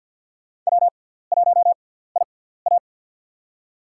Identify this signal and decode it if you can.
Morse code, heard as a keyed tone, about 51 wpm, spaced as for 17 wpm, 700 Hz. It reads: W1IA
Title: USA Call Signs - 6x Character Spacing - 50wpm